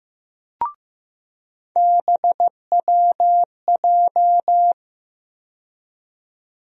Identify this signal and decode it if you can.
Morse code, heard as a keyed tone, about 15 wpm, 700 Hz. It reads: BWJ
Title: USA Call Sign Postfixes - 3 Letters 15wpm